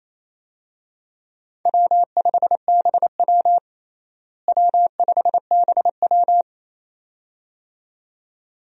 Morse code, heard as a keyed tone, about 27 wpm, 700 Hz. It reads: W5BW W5BW
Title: USA Call Signs - 4 Characters - Repeated 2x - 28wpm